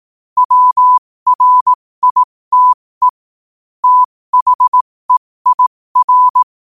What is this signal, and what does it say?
Morse code, heard as a keyed tone, about 18 wpm, spaced as for 15 wpm, 1000 Hz. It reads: WRITE THEIR
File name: qotdfiles / 15wpm